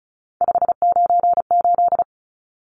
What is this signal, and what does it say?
Morse code, heard as a keyed tone, about 34 wpm, 700 Hz. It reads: 598
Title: Numbers - 3-Digit 35wpm